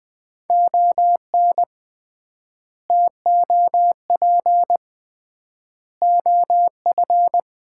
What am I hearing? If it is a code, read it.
Morse code, heard as a keyed tone, about 20 wpm, 700 Hz. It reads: ON TOP OF
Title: 3x Word-Spacing - Sets of 4 Words - 20wpm